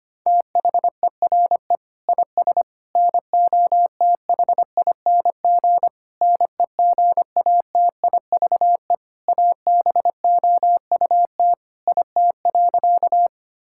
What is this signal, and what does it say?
Morse code, heard as a keyed tone, about 25 wpm, 700 Hz. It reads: THERE IS NOTHING NEGATIVE ABOUT IT.
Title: Sentences from Top 1000 Words - Review Part 6 - 25wpm